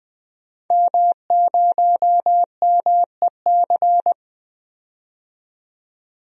Morse code, heard as a keyed tone, about 20 wpm, 700 Hz. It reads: M0MEC
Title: DXCC-100 Call Signs - 5 Characters - 20wpm